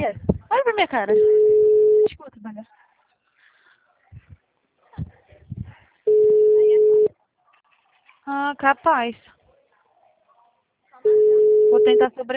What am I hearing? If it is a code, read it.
Ringback tone (Europe, 425 Hz), with speech.